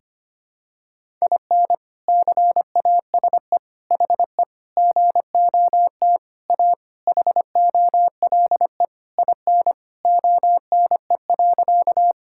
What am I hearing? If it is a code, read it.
Morse code, heard as a keyed tone, about 25 wpm, 700 Hz. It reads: IN CASE HE GOT A HOLE IN ONE.
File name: practice / 25wpm / All-Age Appropriate Jokes - 25wpm